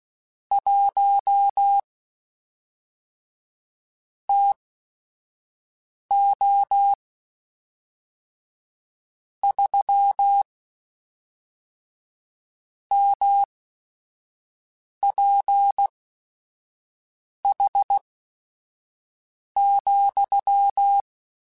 Morse code, heard as a keyed tone, about 16 wpm, spaced as for 5 wpm, 800 Hz. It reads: 1 TO 3 MPH,
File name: Weather Sets / Weather Sets-mp3s / Weather 4 28 @5